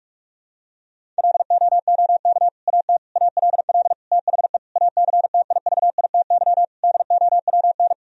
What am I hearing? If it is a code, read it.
Morse code, heard as a keyed tone, about 42 wpm, 700 Hz. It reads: LOOK AT ALL THE ACTIVITY DOWN
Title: Sentences from Top 1000 Words - Review Part 4 - 45wpm